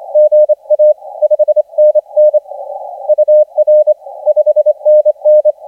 Morse code, heard as a keyed tone, about 26 wpm, 600 Hz, with radio noise. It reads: GA 5NN UR 5NN
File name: ZA DL7UCX 1420UTC 20m